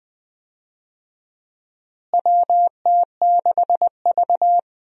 Morse code, heard as a keyed tone, about 20 wpm, 700 Hz. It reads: WT6V
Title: USA Call Signs - 4 Characters - Repeated 3x - 20wpm